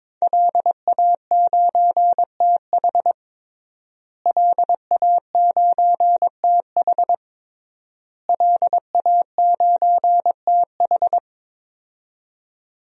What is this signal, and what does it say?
Morse code, heard as a keyed tone, about 22 wpm, 700 Hz. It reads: LA9TH LA9TH LA9TH
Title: DXCC-100 Call Signs - 5 Characters - Repeated 3x - 22wpm